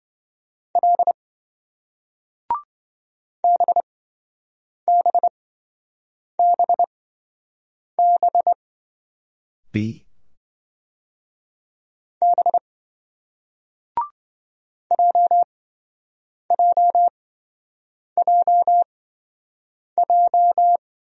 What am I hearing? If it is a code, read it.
Morse code, heard as a keyed tone, about 28 wpm, spaced as for 7 wpm, 700 Hz. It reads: L BBBB B JJJJ